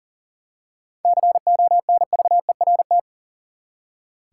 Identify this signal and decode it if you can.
Morse code, heard as a keyed tone, about 40 wpm, 700 Hz. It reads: CONVERT
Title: Seven Letter Words - 40wpm